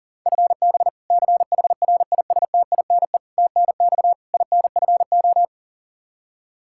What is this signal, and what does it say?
Morse code, heard as a keyed tone, about 40 wpm, 700 Hz. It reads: FB CHRISTINE TNX INFO
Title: Practice QSO - Exchange 3 - More Info - 40wpm